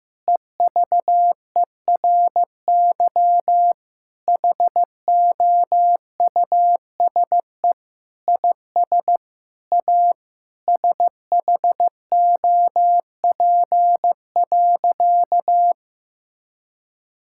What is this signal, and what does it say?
Morse code, heard as a keyed tone, about 15 wpm, 700 Hz. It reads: EVERY HOUSE IS A SHOP.
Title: Sentences from Top 1600 Words 15wpm